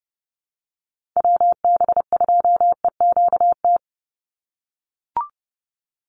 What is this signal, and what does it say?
Morse code, heard as a keyed tone, about 30 wpm, 700 Hz. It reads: WB2EQT
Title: USA Call Signs 30wpm